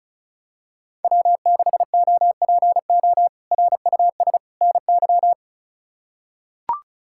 Morse code, heard as a keyed tone, about 34 wpm, 700 Hz. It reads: W6OPO RUS NY
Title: K1USN SST QSO Practice - 35wpm